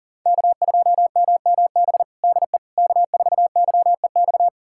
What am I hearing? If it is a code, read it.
Morse code, heard as a keyed tone, about 40 wpm, 700 Hz. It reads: K2MMB DE K4YEX
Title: USA Callsign DE Callsign - 5 Characters - 40wpm